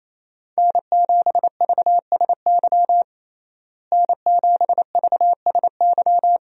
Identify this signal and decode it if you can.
Morse code, heard as a keyed tone, about 28 wpm, 700 Hz. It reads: N7VSY N7VSY
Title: USA Call Signs - 5 Characters - Repeated 3x - 28wpm